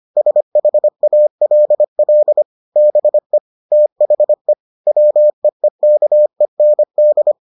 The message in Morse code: SHALL BE THE WEEKEND

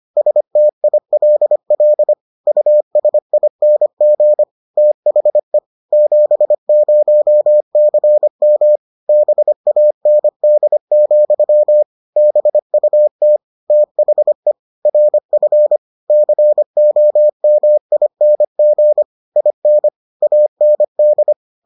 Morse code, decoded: STILL USING THE 70CM BAND, BUT THE RF COMING IN AND

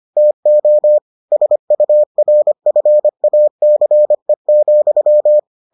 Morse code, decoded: TO SURFACE,

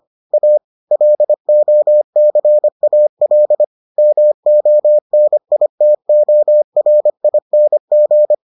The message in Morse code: A LOCAL MONITORING

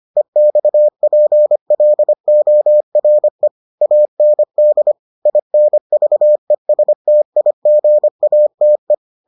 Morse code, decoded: EXPLORE AND INVESTIGATE

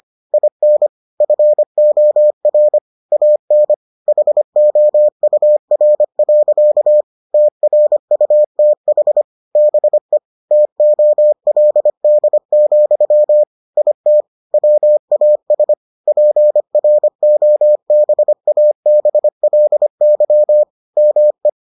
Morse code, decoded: IN FOR AN HOUR. TRUTH BE TOLD, IT WAS PROBABLY ME

